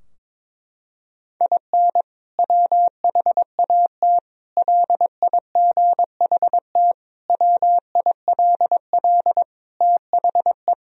Morse code, heard as a keyed tone, about 22 wpm, 700 Hz. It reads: IN WHAT LIGHT WILL THE
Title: Sentences from Top 1000 Words - Review Part 2 - 22wpm